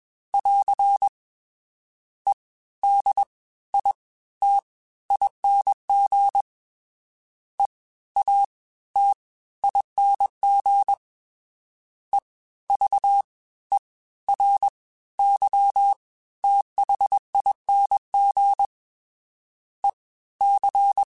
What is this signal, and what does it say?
Morse code, heard as a keyed tone, about 21 wpm, spaced as for 14 wpm, 800 Hz. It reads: <AR> EDITING EATING EVERYTHING EC